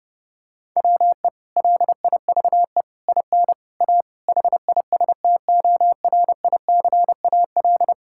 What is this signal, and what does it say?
Morse code, heard as a keyed tone, about 30 wpm, 700 Hz. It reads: WE LIVE IN A HISTORICAL
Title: Sentences from Top 1000 Words - Review Part 5 - 30wpm